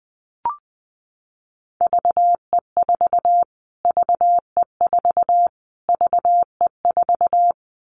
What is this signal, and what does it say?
Morse code, heard as a keyed tone, about 20 wpm, 700 Hz. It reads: VE4 VE4 VE4
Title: Callsign Prefixes for Top 25 DXCC - Repeated 3x - 20wpm